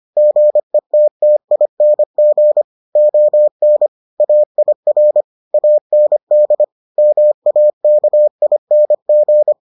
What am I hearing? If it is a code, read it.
Morse code, heard as a keyed tone, about 25 wpm, 600 Hz. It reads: GETTING ON AIR AND MAKING